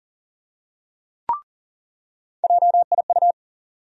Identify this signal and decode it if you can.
Morse code, heard as a keyed tone, about 39 wpm, 700 Hz. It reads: JIU
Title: USA Call Sign Postfixes - 3 Letters 40wpm